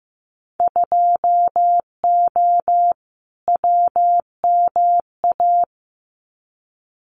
Morse code, heard as a keyed tone, about 15 wpm, 700 Hz. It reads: 2O WMA
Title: Winter Field Day Exchanges Encore 15wpm